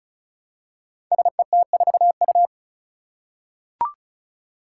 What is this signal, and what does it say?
Morse code, heard as a keyed tone, about 36 wpm, 700 Hz. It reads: SET4U
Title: Five Challenging Letter-Number Combinations 35wpm